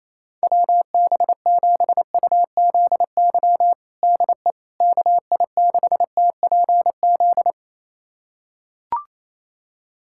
Morse code, heard as a keyed tone, about 28 wpm, 700 Hz. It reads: WB7UZY DE KI6TPZ
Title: USA Callsign DE Callsign - 6 Characters - 28wpm